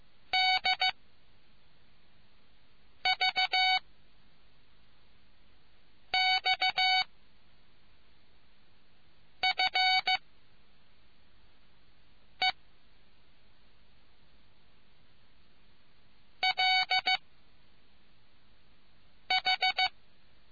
Morse code, heard as a keyed tone, about 15 wpm, spaced as for 4 wpm, 2250 Hz. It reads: DV X F E L H